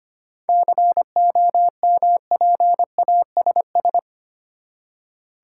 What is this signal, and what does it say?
Morse code, heard as a keyed tone, about 25 wpm, 700 Hz. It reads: COMPASS
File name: practice / 25wpm / Easy Words with Hints - 25wpm